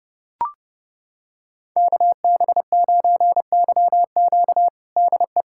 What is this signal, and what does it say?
Morse code, heard as a keyed tone, about 30 wpm, 700 Hz. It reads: KB9YQ DE
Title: USA Callsign DE Callsign 30wpm